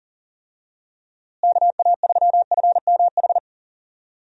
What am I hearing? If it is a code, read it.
Morse code, heard as a keyed tone, about 39 wpm, 700 Hz. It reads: KA3FMH
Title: USA Call Signs - 6 Characters - Repeated 2x - 40wpm